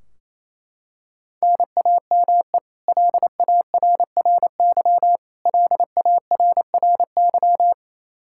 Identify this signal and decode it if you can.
Morse code, heard as a keyed tone, about 28 wpm, 700 Hz. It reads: NAME LARRY LARRY
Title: Practice QSO - Exchange 1 - Beginning - 28wpm